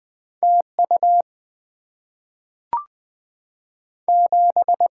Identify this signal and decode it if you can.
Morse code, heard as a keyed tone, about 20 wpm, 700 Hz. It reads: T U 7
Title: Two Letter-Number Combinations - 20wpm